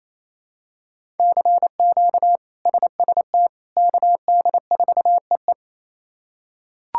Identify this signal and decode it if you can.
Morse code, heard as a keyed tone, about 28 wpm, 700 Hz. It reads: CQ SST KD4EE
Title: K1USN SST QSO Practice - 28wpm